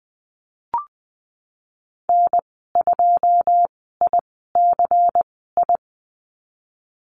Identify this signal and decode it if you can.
Morse code, heard as a keyed tone, about 20 wpm, spaced as for 15 wpm, 700 Hz. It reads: N2ICI
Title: 122 Lesson 20g - Number 3 - Call Signs 20wpm